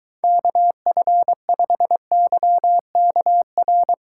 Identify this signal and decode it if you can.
Morse code, heard as a keyed tone, about 22 wpm, 700 Hz. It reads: KF5YKR